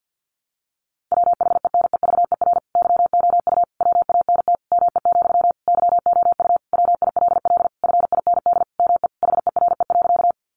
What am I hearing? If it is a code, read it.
Morse code, heard as a keyed tone, about 50 wpm, 700 Hz. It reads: WHENEVER YOU WANT ME, YOU WILL FIND ME HERE.
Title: Sentences from Top 1200 Words 50wpm